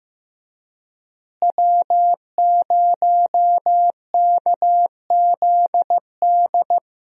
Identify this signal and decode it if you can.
Morse code, heard as a keyed tone, about 15 wpm, 700 Hz. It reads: W0KZD
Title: USA Call Signs - 5 Characters - Repeated 2x - 15wpm